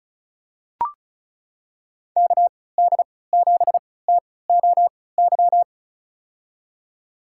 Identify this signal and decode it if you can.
Morse code, heard as a keyed tone, about 36 wpm, spaced as for 20 wpm, 700 Hz. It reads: KD7TOY
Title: USA Call Signs - 6 Characters - 3x Character Spacing - 35wpm